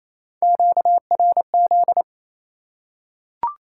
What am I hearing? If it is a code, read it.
Morse code, heard as a keyed tone, about 28 wpm, 700 Hz. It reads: QRZ
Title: Top 90 QSO Elements 28wpm